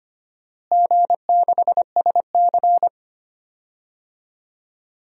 Morse code, heard as a keyed tone, about 25 wpm, 700 Hz. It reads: G6SC
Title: DXCC-100 Call Signs - 4 Characters - Repeated 3x - 25wpm